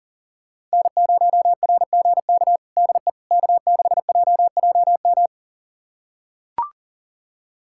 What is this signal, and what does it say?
Morse code, heard as a keyed tone, about 40 wpm, 700 Hz. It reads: N0RGK DE K6JJM
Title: USA Callsign DE Callsign - 5 Characters - 40wpm